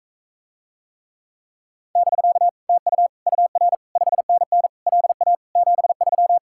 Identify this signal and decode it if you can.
Morse code, heard as a keyed tone, about 43 wpm, 700 Hz. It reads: <BK> TU UR 5NN LA 73